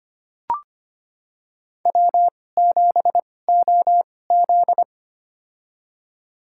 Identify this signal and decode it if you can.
Morse code, heard as a keyed tone, about 25 wpm, spaced as for 18 wpm, 700 Hz. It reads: W7OZ
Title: USA Call Signs - 4 Characters - 2x Character Spacing - 25wpm